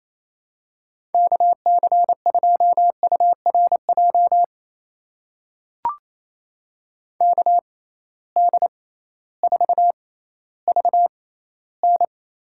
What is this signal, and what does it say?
Morse code, heard as a keyed tone, about 28 wpm, spaced as for 15 wpm, 700 Hz. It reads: KC2URJ KD4VN
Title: USA Call Signs - 6 Characters - 6x Character Spacing - 28wpm